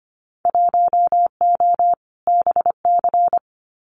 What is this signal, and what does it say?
Morse code, heard as a keyed tone, about 25 wpm, 700 Hz. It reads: 1O BC
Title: Winter Field Day Exchanges Encore 25wpm